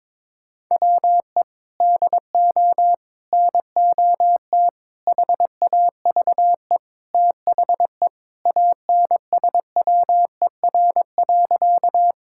Morse code, heard as a keyed tone, about 21 wpm, 700 Hz. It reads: WE DO NOT HAVE THE ANSWER.
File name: practice / 22wpm / Sentences with 4-Word Phrases - 22wpm